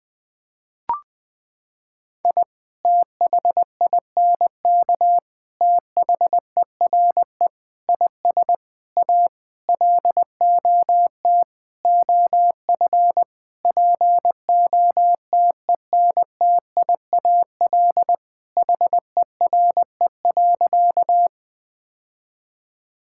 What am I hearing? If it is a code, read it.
Morse code, heard as a keyed tone, about 21 wpm, 700 Hz. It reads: I THINK THERE IS A LOT OF POTENTIAL HERE.
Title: Sentences with 6-Word Phrases - 20wpm